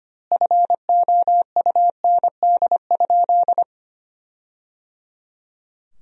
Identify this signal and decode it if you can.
Morse code, heard as a keyed tone, about 25 wpm, 700 Hz. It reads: FOUND?
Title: 2x Word-Spacing - Sentences from Top 300 Words - 25wpm